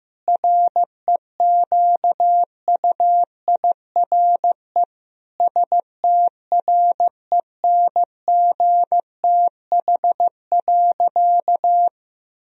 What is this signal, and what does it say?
Morse code, heard as a keyed tone, about 15 wpm, 700 Hz. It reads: REQUIRE STRENGTH.
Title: Sentences from Top 1000 Words - Review Part 4 - 15wpm